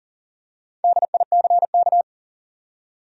Morse code, heard as a keyed tone, about 40 wpm, 700 Hz. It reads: DICK
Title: USA Common Names - Max 5 Letters - 40wpm